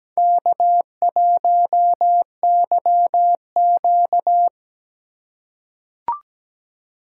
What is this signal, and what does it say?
Morse code, heard as a keyed tone, about 17 wpm, 700 Hz. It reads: K1YQ
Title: USA Call Signs - 4 Characters - 2x Character Spacing - 17wpm